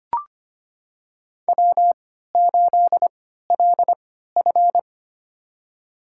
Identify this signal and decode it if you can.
Morse code, heard as a keyed tone, about 25 wpm, spaced as for 14 wpm, 700 Hz. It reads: W8LF
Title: USA Call Signs - 4 Characters - 3x Character Spacing - 25wpm